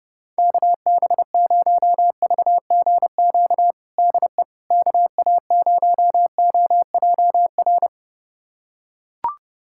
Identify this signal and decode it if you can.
Morse code, heard as a keyed tone, about 30 wpm, 700 Hz. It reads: KB0VGQ DE KA0OJR
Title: USA Callsign DE Callsign 30wpm